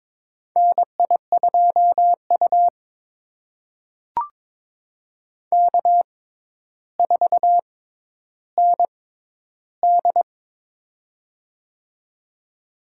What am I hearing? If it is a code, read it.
Morse code, heard as a keyed tone, about 22 wpm, spaced as for 11 wpm, 700 Hz. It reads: NI2U K4ND
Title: USA Call Signs - 4 Characters - 6x Character Spacing - 22wpm